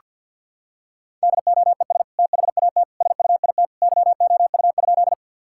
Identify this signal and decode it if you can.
Morse code, heard as a keyed tone, about 48 wpm, 700 Hz. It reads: DOES THAT SUIT YOU?
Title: Sentences from Top 1600 Words 50wpm